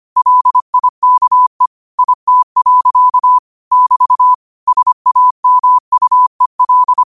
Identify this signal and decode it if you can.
Morse code, heard as a keyed tone, about 25 wpm, 1000 Hz. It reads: LIKE IT. <BT> SAMUEL
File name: qotdfiles / 25wpm